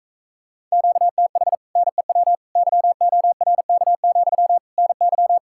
Morse code, heard as a keyed tone, about 45 wpm, 700 Hz. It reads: QTH NEW YORK, NY